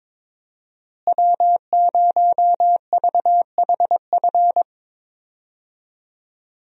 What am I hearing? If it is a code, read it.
Morse code, heard as a keyed tone, about 21 wpm, 700 Hz. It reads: W0VHF
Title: USA Call Signs - 5 Characters - Repeated 3x - 22wpm